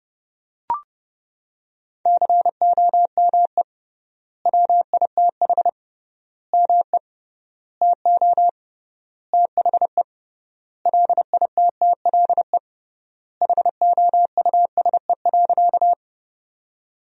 Morse code, heard as a keyed tone, about 30 wpm, 700 Hz. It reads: COME WITH ME TO THE LITTLE HOUSE.
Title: 3x Word-Spacing - Sentences from Top 200 Words - 30wpm